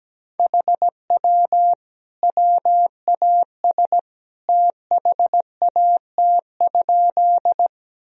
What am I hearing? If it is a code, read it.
Morse code, heard as a keyed tone, about 17 wpm, 700 Hz. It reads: HW WAS THAT?
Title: Sentences from Top 100 Words and Abbreviations 17wpm